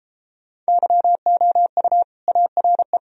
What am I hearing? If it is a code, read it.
Morse code, heard as a keyed tone, about 38 wpm, 700 Hz. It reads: YOU ARE